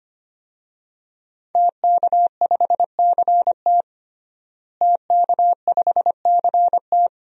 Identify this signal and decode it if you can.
Morse code, heard as a keyed tone, about 25 wpm, 700 Hz. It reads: TK5CT TK5CT
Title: DXCC-100 Call Signs - 5 Characters - Repeated 2x - 25wpm